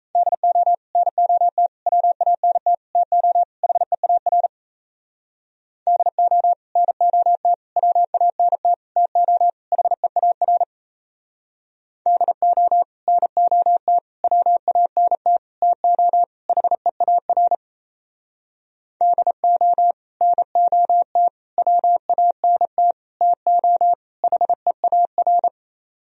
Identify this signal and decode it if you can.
Morse code, heard as a keyed tone, about 37 wpm, 700 Hz. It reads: DO NOT WANT TO HEAR DO NOT WANT TO HEAR DO NOT WANT TO HEAR DO NOT WANT TO HEAR